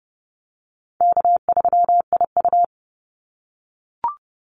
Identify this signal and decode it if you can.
Morse code, heard as a keyed tone, about 30 wpm, 700 Hz. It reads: K3IU
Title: Easy Call Signs - 30wpm